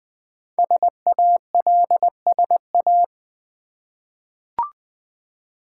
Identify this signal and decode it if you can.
Morse code, heard as a keyed tone, about 20 wpm, 700 Hz. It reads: SALSA
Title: Easy Words with Hints - 20wpm